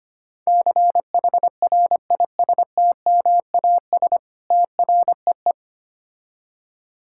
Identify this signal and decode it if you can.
Morse code, heard as a keyed tone, about 25 wpm, 700 Hz. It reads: CHRISTMAS TREE
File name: Easy Words - Speed-Racing with Hints - 25wpm